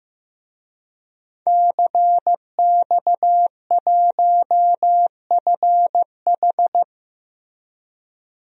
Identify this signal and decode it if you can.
Morse code, heard as a keyed tone, about 15 wpm, 700 Hz. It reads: CX1FH
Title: DXCC-100 Call Signs - 5 Characters - Repeated 2x - 15wpm